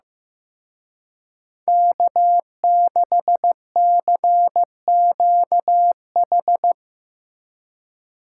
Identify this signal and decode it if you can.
Morse code, heard as a keyed tone, about 17 wpm, 700 Hz. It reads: K6CQH
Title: Speed Racing - USA 1x3 Call Signs 15wpm